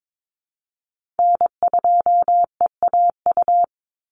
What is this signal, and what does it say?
Morse code, heard as a keyed tone, about 22 wpm, 700 Hz. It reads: N2EAU
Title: USA Call Signs 22wpm